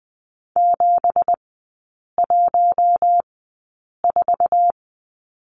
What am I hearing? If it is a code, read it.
Morse code, heard as a keyed tone, about 19 wpm, spaced as for 8 wpm, 690 Hz. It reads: 714
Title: 166 Lesson 27e - Number 7 - Three Characters 20wpm